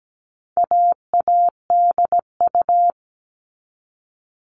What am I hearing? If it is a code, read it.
Morse code, heard as a keyed tone, about 17 wpm, 700 Hz. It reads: AADU
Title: All Cut Numbers - 4-Digit 17wpm